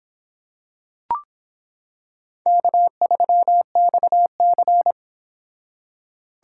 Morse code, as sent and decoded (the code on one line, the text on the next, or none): -.- ...-- -..- -.-.
K3XC